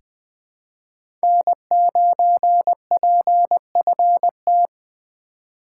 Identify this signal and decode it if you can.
Morse code, heard as a keyed tone, about 20 wpm, 700 Hz. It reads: N9PFT
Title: USA Call Signs - 5 Characters - 2x Character Spacing - 20wpm